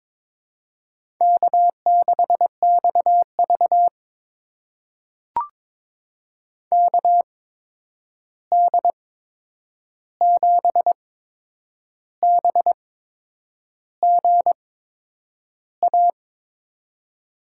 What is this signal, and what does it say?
Morse code, heard as a keyed tone, about 22 wpm, spaced as for 8 wpm, 700 Hz. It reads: K6XV KD7BGA
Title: USA Call Signs - 8x Character Spacing - 22wpm